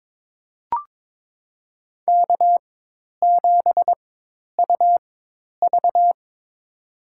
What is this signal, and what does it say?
Morse code, heard as a keyed tone, about 23 wpm, spaced as for 10 wpm, 700 Hz. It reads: K7UV
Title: USA Call Signs - 4 Characters - 4x Character Spacing - 22wpm